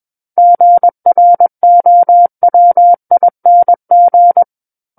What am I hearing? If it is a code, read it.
Morse code, heard as a keyed tone, about 21 wpm, 700 Hz. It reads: GROWING
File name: Book One / 12-Chapter 12 - 21 WPM